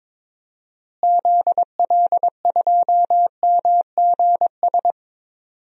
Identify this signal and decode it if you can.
Morse code, heard as a keyed tone, about 22 wpm, 700 Hz. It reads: ZL2MGS
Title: DXCC-100 Call Signs - 6 Characters - Repeated 2x - 22wpm